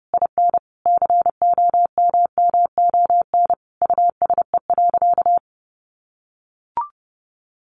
Morse code, heard as a keyed tone, about 30 wpm, 700 Hz. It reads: IN COMMON USE.
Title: Sentences from Top 1300 Words 30wpm